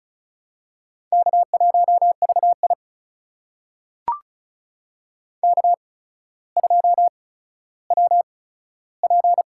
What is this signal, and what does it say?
Morse code, heard as a keyed tone, about 36 wpm, spaced as for 15 wpm, 700 Hz. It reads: K1VI K2WP
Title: USA Call Signs - 4 Characters - 8x Character Spacing - 35wpm